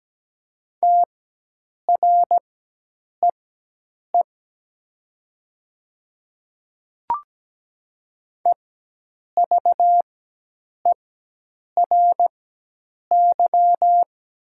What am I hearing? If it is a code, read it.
Morse code, heard as a keyed tone, about 17 wpm, spaced as for 8 wpm, 700 Hz. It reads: TREE EVERY
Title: 4x Character Spacing - Top 300 Words - 17wpm